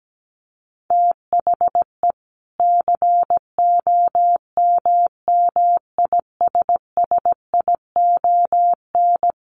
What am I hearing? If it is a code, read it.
Morse code, heard as a keyed tone, about 17 wpm, 700 Hz. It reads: THE COMMISSION
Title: Sentences from Top 1400 Words 17wpm